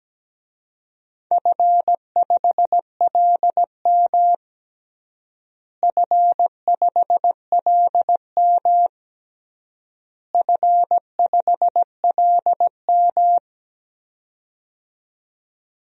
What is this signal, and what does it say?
Morse code, heard as a keyed tone, about 17 wpm, 700 Hz. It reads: F5LM F5LM F5LM
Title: DXCC-100 Call Signs - 4 Characters - Repeated 3x - 17wpm